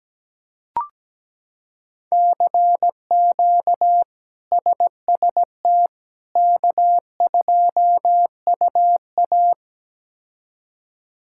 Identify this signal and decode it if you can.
Morse code, heard as a keyed tone, about 17 wpm, 700 Hz. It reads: CQ SST K2UA
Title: K1USN SST QSO Practice - 17wpm